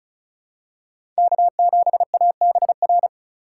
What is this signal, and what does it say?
Morse code, heard as a keyed tone, about 36 wpm, 700 Hz. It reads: K7ABR
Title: USA Call Signs - 5 Characters - 2x Character Spacing - 35wpm